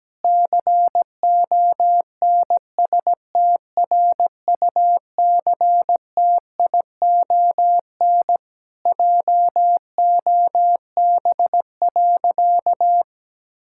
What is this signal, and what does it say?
Morse code, heard as a keyed tone, about 17 wpm, 700 Hz. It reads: CONSTRUCTION JOB.
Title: Sentences from Top 1600 Words 17wpm